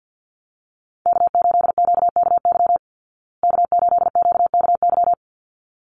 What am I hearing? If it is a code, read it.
Morse code, heard as a keyed tone, about 48 wpm, 700 Hz. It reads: K8QKY K8QKY
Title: Call Signs Repeated 3x - 50wpm